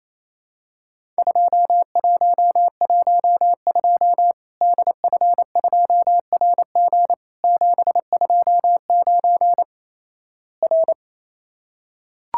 Morse code, heard as a keyed tone, about 28 wpm, 700 Hz. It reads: 2112 DF2RG 729 R
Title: Worked All Europe DX Contest - QTC Exchange - 28wpm